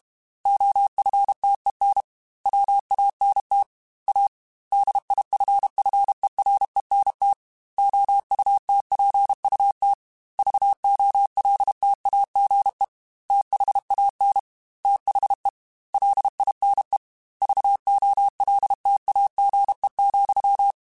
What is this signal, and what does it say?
Morse code, heard as a keyed tone, about 32 wpm, 800 Hz. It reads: OFTEN WANT A DIFFERENT OUTPUT VOLTAGE THAN THE LINE VOLTAGE,
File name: Technical Subjects / Technical Subjects-mp3s / Sw Pwr Sup @30